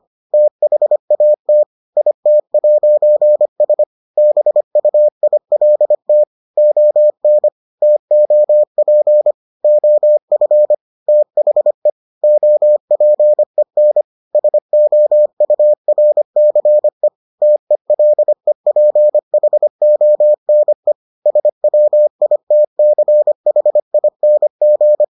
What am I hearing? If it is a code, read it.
Morse code, heard as a keyed tone, about 25 wpm, 600 Hz. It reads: THAT IT'S BUILT ON TOP OF THE OPEN SOURCE TELEPHONE SWITCHING